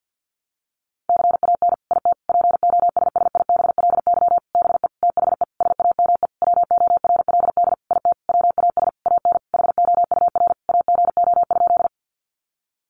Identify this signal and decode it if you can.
Morse code, heard as a keyed tone, about 49 wpm, 700 Hz. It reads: CAN IT POSSIBLY BE THE SAME WORLD IT WAS AN HOUR AGO?
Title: Sentences from Top 1300 Words 50wpm